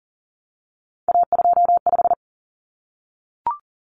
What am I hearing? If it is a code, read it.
Morse code, heard as a keyed tone, about 40 wpm, 700 Hz. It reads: A25
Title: Common Cut Numbers - 3-Digit 40wpm